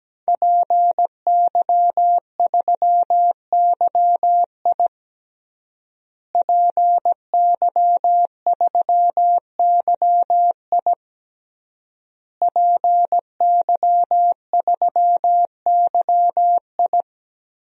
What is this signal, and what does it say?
Morse code, heard as a keyed tone, about 17 wpm, 700 Hz. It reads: PY3YI PY3YI PY3YI
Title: DXCC-100 Call Signs - 5 Characters - Repeated 3x - 17wpm